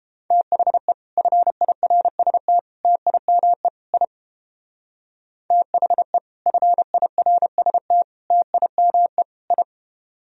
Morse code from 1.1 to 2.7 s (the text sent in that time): FIRST